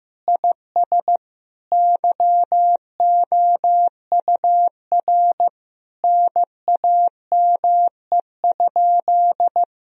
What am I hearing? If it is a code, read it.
Morse code, heard as a keyed tone, about 15 wpm, 700 Hz. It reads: IS YOUR NAME?
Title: Sentences from Top 1000 Words - Review Part 3 - 15wpm